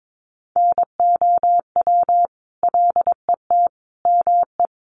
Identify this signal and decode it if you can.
Morse code, heard as a keyed tone, about 22 wpm, 700 Hz. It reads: NOW LET ME
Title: Sentences from Top 1200 Words 22wpm